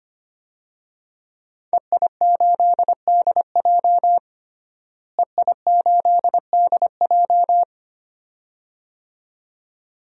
Morse code, heard as a keyed tone, about 25 wpm, 700 Hz. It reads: EI8DJ EI8DJ
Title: DXCC-100 Call Signs - 5 Characters - Repeated 2x - 25wpm